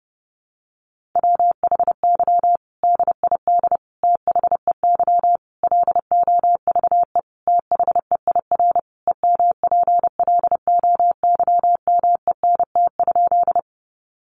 Morse code, heard as a keyed tone, about 28 wpm, 700 Hz. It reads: WHY DID THEY LOVE THEIR EMPLOYMENT?
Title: Sentences from Top 1500 Words 30wpm